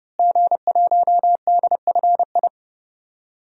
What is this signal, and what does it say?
Morse code, heard as a keyed tone, about 32 wpm, 700 Hz. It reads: G1DFI
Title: DXCC-100 Call Signs - 5 Characters - Repeated 3x - 30wpm